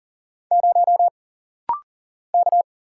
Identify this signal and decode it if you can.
Morse code, heard as a keyed tone, about 39 wpm, 700 Hz. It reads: 0 K
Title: Single Letter-Number - Rapid-Fire 40wpm